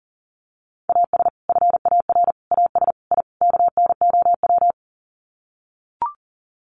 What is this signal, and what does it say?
Morse code, heard as a keyed tone, about 40 wpm, 700 Hz. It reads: AS FAR AS I KNOW
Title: Sets of 5 Words 40wpm